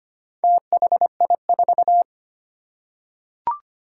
Morse code, heard as a keyed tone, about 25 wpm, 700 Hz. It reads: THI4
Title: Four Challenging Letter-Number Combinations 25wpm